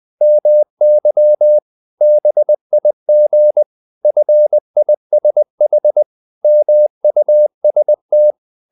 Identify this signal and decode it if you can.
Morse code, heard as a keyed tone, about 20 wpm, 600 Hz. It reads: MY BIG FISH MUST
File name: the old man and the sea / the old man and the sea part 1 20wpm